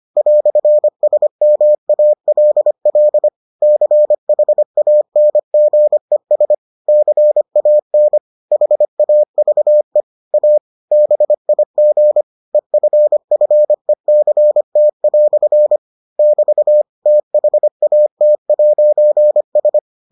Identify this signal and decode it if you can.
Morse code, heard as a keyed tone, about 25 wpm, 600 Hz. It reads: "SMALL CHANGES CAN HAVE A BIG EFFECT" <BT> THAT'S